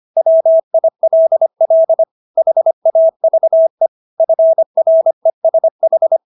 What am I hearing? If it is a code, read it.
Morse code, heard as a keyed tone, about 25 wpm, 650 Hz. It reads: WILL HAVE FRESH